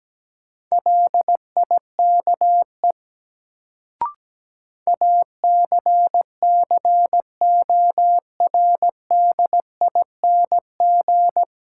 Morse code, heard as a keyed tone, about 17 wpm, 700 Hz. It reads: LIKE ACCORDING
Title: Sets of 2 Words - New Word Drill - 17wpm